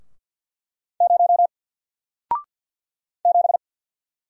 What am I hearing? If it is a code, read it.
Morse code, heard as a keyed tone, about 48 wpm, 700 Hz. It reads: 0 7
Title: Single Character (Letter:Number:Punctuation:Prosign) Alt 50wpm